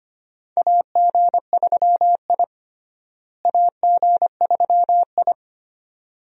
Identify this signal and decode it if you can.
Morse code, heard as a keyed tone, about 25 wpm, 700 Hz. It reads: AG3I AG3I
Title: USA Call Signs - 4 Characters - Repeated 2x - 25wpm